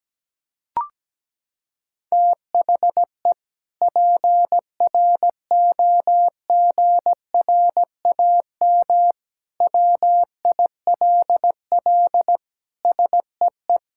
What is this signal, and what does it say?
Morse code, heard as a keyed tone, about 17 wpm, 700 Hz. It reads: THE PROGRAM WILL SEE
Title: Sentences from Top 1000 Words - Review Part 5 - 17wpm